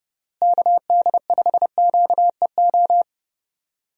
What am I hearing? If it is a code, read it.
Morse code, heard as a keyed tone, about 32 wpm, 700 Hz. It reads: KD5QEO
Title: USA Call Signs - 6 Characters - Repeated 3x - 30wpm